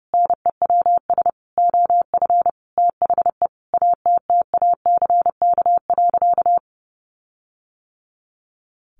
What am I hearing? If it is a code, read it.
Morse code, heard as a keyed tone, about 30 wpm, 700 Hz. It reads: NEWS OF THE ATTACK.
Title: Sentences from Top 1500 Words 30wpm